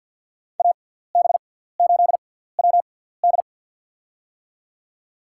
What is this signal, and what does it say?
Morse code, heard as a keyed tone, about 47 wpm, spaced as for 17 wpm, 700 Hz. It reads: AB8WD
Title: USA Call Signs - 6x Character Spacing - 50wpm